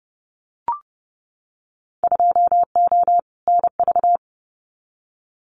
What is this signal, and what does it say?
Morse code, heard as a keyed tone, about 30 wpm, 700 Hz. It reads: 2O NV
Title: Winter Field Day Exchanges 30wpm